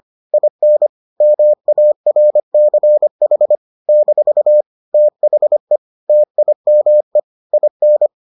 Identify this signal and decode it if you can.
Morse code, heard as a keyed tone, about 25 wpm, 600 Hz. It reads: IN MARCH - THE TIME IN